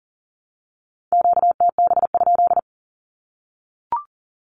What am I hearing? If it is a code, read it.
Morse code, heard as a keyed tone, about 40 wpm, 700 Hz. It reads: QTB?
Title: Q-Codes Repeated 3x 40wpm